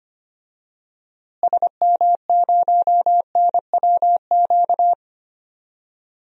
Morse code, heard as a keyed tone, about 25 wpm, 700 Hz. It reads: SM0NWQ
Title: DXCC-100 Call Signs - 6 Characters - Repeated 3x - 25wpm